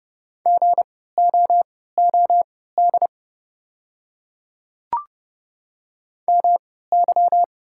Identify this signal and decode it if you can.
Morse code, heard as a keyed tone, about 30 wpm, spaced as for 17 wpm, 700 Hz. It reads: GOOD MY
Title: 3x Character Spacing - Top 500 Words - 30wpm